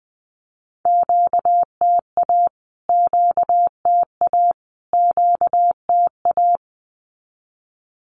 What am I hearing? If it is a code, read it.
Morse code, heard as a keyed tone, about 20 wpm, 700 Hz. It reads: QTA QTA QTA
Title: Q-Codes Repeated 3x 20wpm